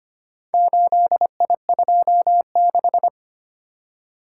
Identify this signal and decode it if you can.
Morse code, heard as a keyed tone, about 25 wpm, 700 Hz. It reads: 8I26
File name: practice / 25wpm / Four Letter-Number Combinations - 25wpm